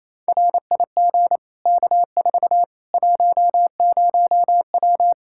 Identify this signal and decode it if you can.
Morse code, heard as a keyed tone, about 28 wpm, 700 Hz. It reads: RIG K4 10W
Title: Practice QSO - Exchange 2 - Info - 28wpm